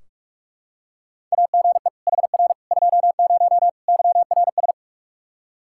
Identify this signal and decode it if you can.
Morse code, heard as a keyed tone, about 46 wpm, 700 Hz. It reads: AGE HR 20 YRS
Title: K1USN SST QSO Practice 45wpm